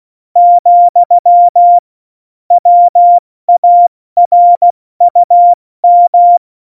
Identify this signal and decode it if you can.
Morse code, heard as a keyed tone, about 16 wpm, 700 Hz. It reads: , WARUM
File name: Morsekurs / Witze / Witz 14 Juli 2020, 70 F80